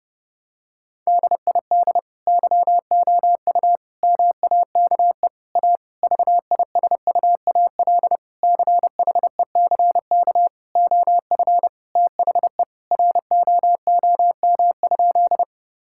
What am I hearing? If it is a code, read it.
Morse code, heard as a keyed tone, about 30 wpm, 700 Hz. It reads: DID YOU MAKE A VISUAL CHECK OF THE ROOM?
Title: Sentences from Top 1000 Words - Review Part 6 - 30wpm